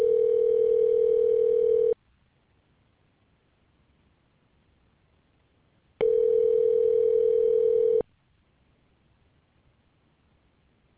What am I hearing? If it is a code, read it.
Ringback tone (North America), 440 + 480 Hz.